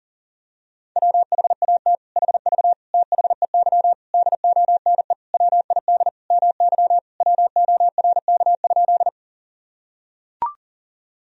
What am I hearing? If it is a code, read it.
Morse code, heard as a keyed tone, about 37 wpm, 700 Hz. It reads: WHAT HV THEY DONE WID MY WORK?
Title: Sentences from Top 500 Words and Abbreviations 40wpm